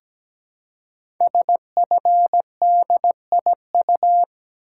Morse code, heard as a keyed tone, about 17 wpm, 700 Hz. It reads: SFDIU
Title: Code Groups - 5 Letters - 17wpm